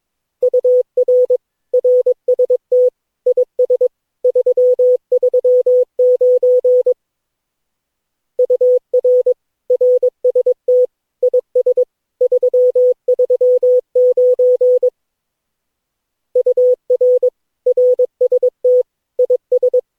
Morse code, heard as a keyed tone, about 22 wpm, 500 Hz. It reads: UR RST IS 339 UR RST IS 339 UR RST IS
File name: cw / Session9 cw training